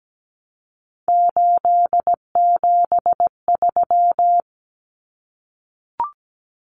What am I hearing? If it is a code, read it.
Morse code, heard as a keyed tone, about 17 wpm, 700 Hz. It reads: 873
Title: Common Cut Numbers - 3-Digit 17wpm